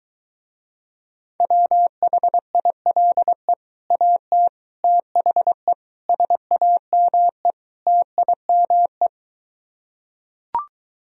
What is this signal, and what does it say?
Morse code, heard as a keyed tone, about 23 wpm, 700 Hz. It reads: WHILE AT THE SAME TIME